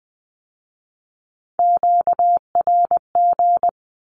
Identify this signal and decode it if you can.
Morse code, heard as a keyed tone, about 20 wpm, 700 Hz. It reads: QRG
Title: Q-Codes with Spoken Meaning - 20wpm